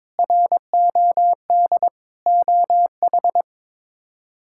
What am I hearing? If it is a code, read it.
Morse code, heard as a keyed tone, about 21 wpm, 700 Hz. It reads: ROD OH
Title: K1USN SST QSO Practice - 22wpm